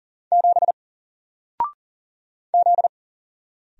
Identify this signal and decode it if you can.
Morse code, heard as a keyed tone, about 40 wpm, 700 Hz. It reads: 7 Z
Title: Single Letter-Number Alt 40wpm